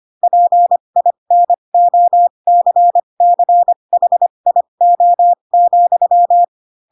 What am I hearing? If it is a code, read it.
Morse code, heard as a keyed tone, about 25 wpm, spaced as for 22 wpm, 700 Hz. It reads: PINOCCHIO,
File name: Pinocchio / Pinocchio ch7 18 25 ik1hge